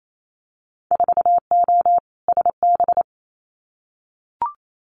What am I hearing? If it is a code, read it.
Morse code, heard as a keyed tone, about 28 wpm, 700 Hz. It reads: 4O SB
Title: Winter Field Day Exchanges 28wpm